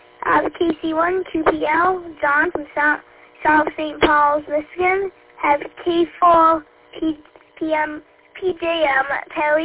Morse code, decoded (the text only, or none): none